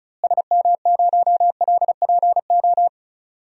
Morse code, heard as a keyed tone, about 36 wpm, 700 Hz. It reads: SM0LPO
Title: DXCC-100 Call Signs - 6 Characters - Repeated 2x - 35wpm